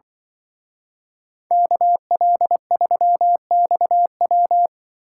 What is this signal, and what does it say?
Morse code, heard as a keyed tone, about 24 wpm, 700 Hz. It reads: KL3XW
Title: Speed Racing - USA Call Signs 20wpm